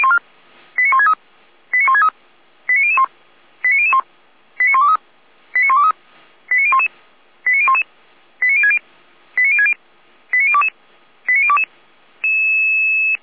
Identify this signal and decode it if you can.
Five-tone selective call (ZVEI), heard as quick runs of five tones, tones of about 70 ms. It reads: 89163 89163 89001 89001 89123 89123 89011 89011 89060 89060 89022 89022